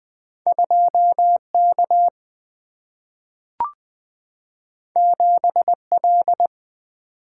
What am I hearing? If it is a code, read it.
Morse code, heard as a keyed tone, about 20 wpm, 700 Hz. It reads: 2K 7L
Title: Single-Letter Single-Number 20wpm